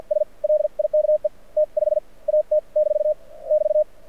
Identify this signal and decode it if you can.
Morse code, heard as a keyed tone, about 54 wpm, spaced as for 39 wpm, 600 Hz. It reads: S LIKE T5AT - <BT>
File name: archived / CRSnet